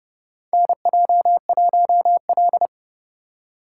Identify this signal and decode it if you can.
Morse code, heard as a keyed tone, about 30 wpm, 700 Hz. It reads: NJ1L
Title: USA Call Signs - 4 Characters - 3x Character Spacing - 30wpm